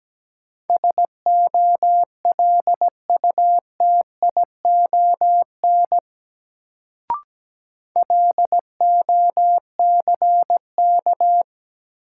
Morse code, heard as a keyed tone, about 17 wpm, 700 Hz. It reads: SOLUTION LOCK
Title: Sentences from Top 1300 Words - New Word Drill - 17wpm